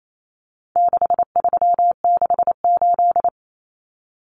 Morse code, heard as a keyed tone, about 28 wpm, 700 Hz. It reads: 6368
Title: Common Cut Numbers - 4-Digit 28wpm